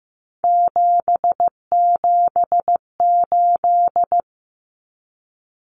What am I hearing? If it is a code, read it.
Morse code, heard as a keyed tone, about 15 wpm, 700 Hz. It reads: 778
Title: Numbers - 3-Digit 15wpm